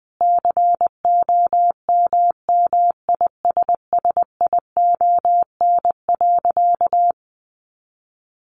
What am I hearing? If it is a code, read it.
Morse code, heard as a keyed tone, about 20 wpm, 700 Hz. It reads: COMMISSION.
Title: Sentences from Top 1400 Words 20wpm